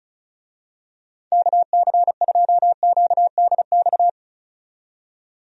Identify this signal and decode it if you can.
Morse code, heard as a keyed tone, about 34 wpm, 700 Hz. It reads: KC2QDX
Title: USA Call Signs - 6 Characters - 35wpm